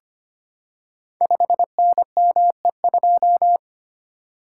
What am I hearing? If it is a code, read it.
Morse code, heard as a keyed tone, about 25 wpm, 700 Hz. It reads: 5NME2
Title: Code Groups - 5 Letter:Numbers - 25wpm